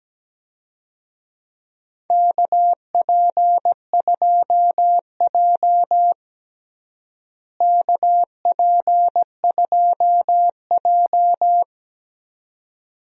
Morse code, heard as a keyed tone, about 17 wpm, 700 Hz. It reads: KP2J KP2J
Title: DXCC-100 Call Signs - 4 Characters - Repeated 2x - 17wpm